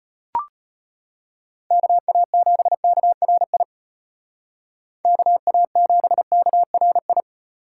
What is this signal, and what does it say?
Morse code, heard as a keyed tone, about 32 wpm, 700 Hz. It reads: KA7KRI KA7KRI